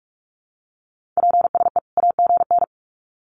Morse code, heard as a keyed tone, about 45 wpm, 700 Hz. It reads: PSE AGN
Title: Common QSO Phrases 45wpm